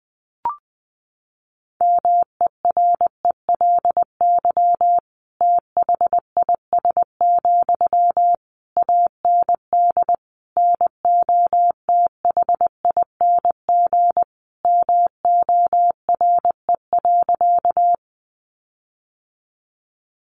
Morse code, heard as a keyed tone, about 20 wpm, 700 Hz. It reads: MERELY THIS, AND NOTHING MORE.
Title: Sentences from Top 1400 Words 20wpm